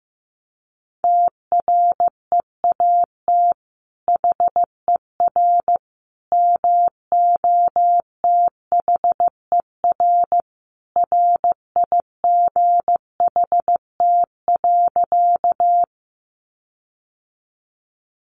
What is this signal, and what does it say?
Morse code, heard as a keyed tone, about 15 wpm, 700 Hz. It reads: TREAT HER MOTHER RIGHT.
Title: Sentences from Top 1300 Words 15wpm